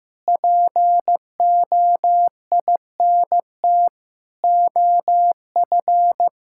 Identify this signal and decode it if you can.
Morse code, heard as a keyed tone, about 15 wpm, 700 Hz. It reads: POINT OF
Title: Sentences from Top 1000 Words - Review Part 6 - 15wpm